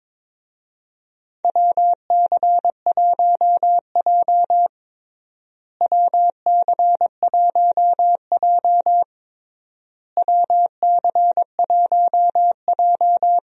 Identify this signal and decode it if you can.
Morse code, heard as a keyed tone, about 22 wpm, 700 Hz. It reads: WC1J WC1J WC1J
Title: USA Call Signs - 4 Characters - Repeated 3x - 22wpm